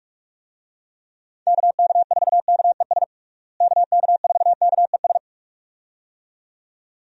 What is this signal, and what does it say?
Morse code, heard as a keyed tone, about 41 wpm, 700 Hz. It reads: KK4KES KK4KES
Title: USA Call Signs - 6 Characters - Repeated 2x - 45wpm